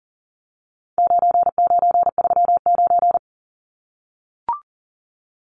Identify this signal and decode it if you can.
Morse code, heard as a keyed tone, about 40 wpm, 700 Hz. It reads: 9939
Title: Numbers - 4-Digit 40wpm